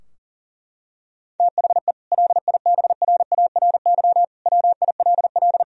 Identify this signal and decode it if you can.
Morse code, heard as a keyed tone, about 40 wpm, 700 Hz. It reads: THE LIBRARY WILL
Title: Sentences from Top 1000 Words - Review Part 4 - 40wpm